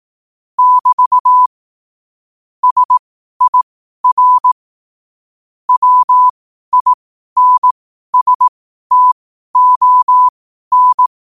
Morse code, heard as a keyed tone, about 18 wpm, spaced as for 12 wpm, 1000 Hz. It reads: <BT> SIR WINSTON
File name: qotdfiles / 12wpm